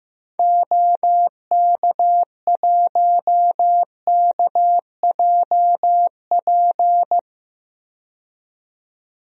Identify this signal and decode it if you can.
Morse code, heard as a keyed tone, about 15 wpm, 700 Hz. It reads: OK1KJP
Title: DXCC-100 Call Signs - 6 Characters - 15wpm